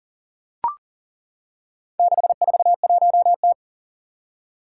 Morse code, heard as a keyed tone, about 40 wpm, 700 Hz. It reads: 641T
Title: Four Challenging Letter-Number Combinations 40wpm